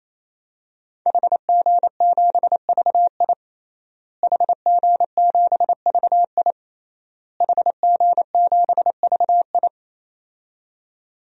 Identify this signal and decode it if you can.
Morse code, heard as a keyed tone, about 28 wpm, 700 Hz. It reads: HG7VI HG7VI HG7VI
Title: DXCC-100 Call Signs - 5 Characters - Repeated 3x - 28wpm